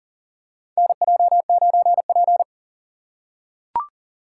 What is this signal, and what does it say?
Morse code, heard as a keyed tone, about 40 wpm, 700 Hz. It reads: NJ9P
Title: USA Call Signs - 4 Characters - 6x Character Spacing - 40wpm